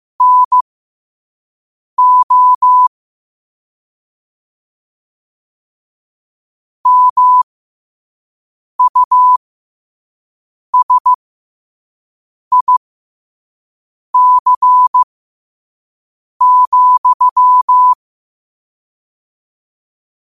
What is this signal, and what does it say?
Morse code, heard as a keyed tone, about 15 wpm, spaced as for 5 wpm, 1000 Hz. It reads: NO MUSIC,
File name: qotdfiles / 5wpm